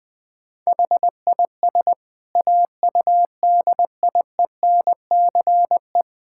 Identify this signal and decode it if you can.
Morse code, heard as a keyed tone, about 21 wpm, 700 Hz. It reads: HIS AUDIENCE
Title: Sentences from Top 1000 Words - Review Part 5 - 20wpm